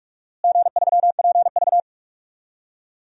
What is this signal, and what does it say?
Morse code, heard as a keyed tone, about 44 wpm, 700 Hz. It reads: G3PV
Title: DXCC-100 Call Signs - 4 Characters - Repeated 3x - 45wpm